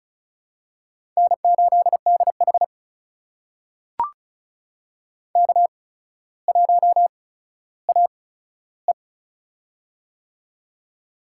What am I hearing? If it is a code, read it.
Morse code, heard as a keyed tone, about 36 wpm, spaced as for 15 wpm, 700 Hz. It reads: N8DH K1AE
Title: USA Call Signs - 4 Characters - 8x Character Spacing - 35wpm